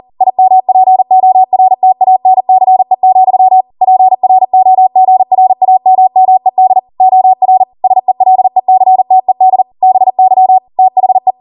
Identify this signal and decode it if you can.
Morse code, heard as a keyed tone, about 40 wpm, 750 Hz. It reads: IMPORTANCE, PROGRAMMED OR SELECTED BY THE
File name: Morse / 40WPM